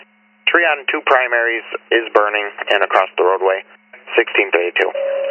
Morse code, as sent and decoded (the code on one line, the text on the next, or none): none